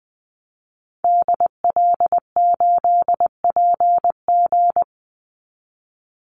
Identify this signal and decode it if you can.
Morse code, heard as a keyed tone, about 20 wpm, 700 Hz. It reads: DL8PG
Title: Call Signs Repeated 2x - 20wpm